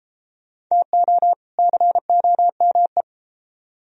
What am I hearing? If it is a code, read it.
Morse code, heard as a keyed tone, about 33 wpm, 700 Hz. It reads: TO COME